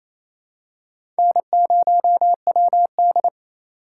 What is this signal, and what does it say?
Morse code, heard as a keyed tone, about 28 wpm, 700 Hz. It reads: N0WD
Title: USA Call Signs - 4 Characters - 2x Character Spacing - 28wpm